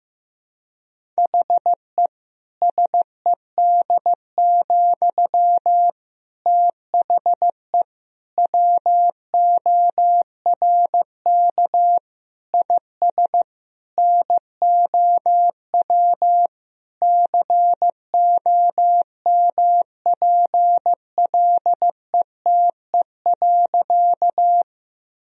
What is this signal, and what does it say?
Morse code, heard as a keyed tone, about 15 wpm, 700 Hz. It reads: HE SED, THE WORK IS NOW COMPLETE.
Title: Sentences from Top 400 Words and Abbreviations 15wpm